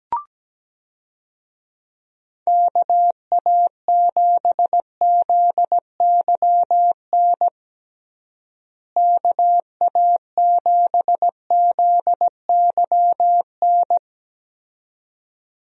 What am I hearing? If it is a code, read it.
Morse code, heard as a keyed tone, about 17 wpm, 700 Hz. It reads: KA7ZYN KA7ZYN
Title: USA Call Signs - 6 Characters - Repeated 2x - 17wpm